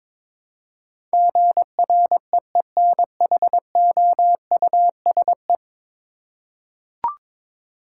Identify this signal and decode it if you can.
Morse code, heard as a keyed tone, about 22 wpm, 700 Hz. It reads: GREENHOUSE
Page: morse-code-ninja-practice-22wpm